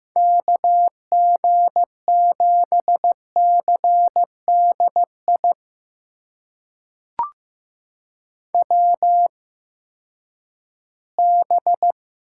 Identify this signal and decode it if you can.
Morse code, heard as a keyed tone, about 15 wpm, spaced as for 10 wpm, 700 Hz. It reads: KG7CDI WB
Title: USA Call Signs - 6 Characters - 8x Character Spacing - 15wpm